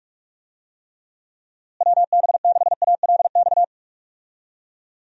Morse code, heard as a keyed tone, about 44 wpm, 700 Hz. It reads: WB6ALX
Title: USA Call Signs - 6 Characters - Repeated 3x - 45wpm